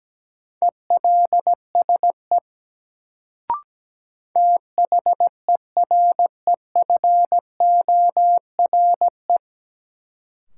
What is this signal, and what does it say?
Morse code, heard as a keyed tone, about 17 wpm, 700 Hz. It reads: ELSE THEREFORE
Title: Sentences from Top 800 Words - New Word Drill - 17wpm